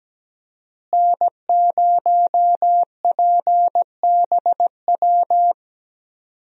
Morse code, heard as a keyed tone, about 17 wpm, 700 Hz. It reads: N0PBW
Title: USA Call Signs - 5 Characters - Repeated 2x - 17wpm